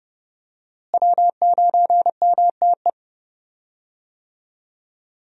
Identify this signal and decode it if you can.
Morse code, heard as a keyed tone, about 32 wpm, 700 Hz. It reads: W9MTE
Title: USA Call Signs - 5 Characters - Repeated 2x - 30wpm